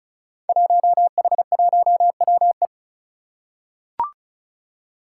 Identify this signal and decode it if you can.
Morse code, heard as a keyed tone, about 36 wpm, 700 Hz. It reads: 1H1WE
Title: Five Challenging Letter-Number Combinations 35wpm